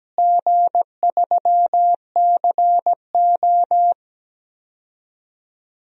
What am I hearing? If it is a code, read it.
Morse code, heard as a keyed tone, about 17 wpm, 700 Hz. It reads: G3CO
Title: DXCC-100 Call Signs - 4 Characters - Repeated 2x - 17wpm